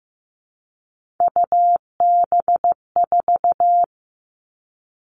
Morse code, heard as a keyed tone, about 15 wpm, 700 Hz. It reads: UB4
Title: All Cut Numbers - 3-Digit 15wpm